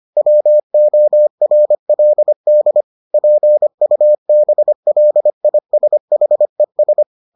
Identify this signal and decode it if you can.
Morse code, heard as a keyed tone, about 25 wpm, 600 Hz. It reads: WORLD PUBLISHES